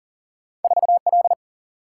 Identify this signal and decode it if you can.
Morse code, heard as a keyed tone, about 40 wpm, 700 Hz. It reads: 4L
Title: Single-Letter Single-Number 40wpm